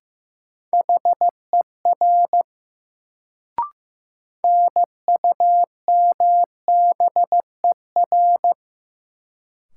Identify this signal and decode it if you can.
Morse code, heard as a keyed tone, about 15 wpm, 700 Hz. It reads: HER NUMBER
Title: Sets of 2 Words - New Word Drill - 15wpm